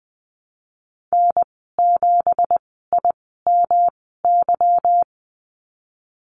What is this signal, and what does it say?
Morse code, heard as a keyed tone, about 20 wpm, spaced as for 15 wpm, 700 Hz. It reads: N7IMY
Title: 173 Lesson 28g - Number 9 - Call Signs 20wpm